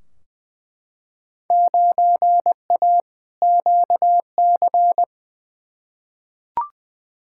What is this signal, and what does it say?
Morse code, heard as a keyed tone, about 20 wpm, 700 Hz. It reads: 9A QC
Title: Field Day Exchanges Encore - 20wpm